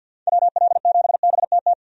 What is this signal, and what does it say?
Morse code, heard as a keyed tone, about 49 wpm, 700 Hz. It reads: WL7BTT
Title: USA Call Signs - 6 Characters - 50wpm